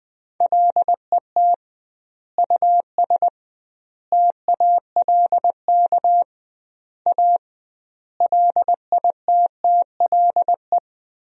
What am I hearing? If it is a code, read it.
Morse code, heard as a keyed tone, about 21 wpm, 700 Hz. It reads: LET US TALK A LITTLE
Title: 2x Word-Spacing - Sentences from Top 400 Words - 20wpm